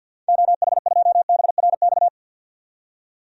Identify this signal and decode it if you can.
Morse code, heard as a keyed tone, about 51 wpm, 700 Hz. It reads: OH2BRX
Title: DXCC-100 Call Signs - 6 Characters - 50wpm